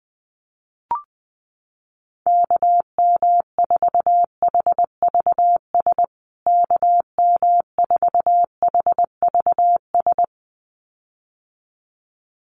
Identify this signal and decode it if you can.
Morse code, heard as a keyed tone, about 20 wpm, 700 Hz. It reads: KM4HVS KM4HVS
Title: USA Call Signs Repeated 2x 20wpm